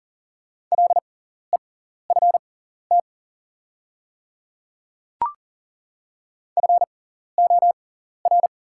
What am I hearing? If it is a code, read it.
Morse code, heard as a keyed tone, about 38 wpm, spaced as for 14 wpm, 700 Hz. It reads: LEFT FOR
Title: 6x Character Spacing - Top 500 Words - 40wpm